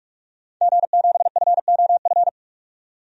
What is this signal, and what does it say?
Morse code, heard as a keyed tone, about 41 wpm, 700 Hz. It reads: G7FOF
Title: DXCC-100 Call Signs - 5 Characters - 45wpm